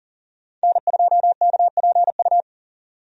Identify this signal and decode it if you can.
Morse code, heard as a keyed tone, about 40 wpm, 700 Hz. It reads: N2KPU
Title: USA Call Signs - 5 Characters - 40wpm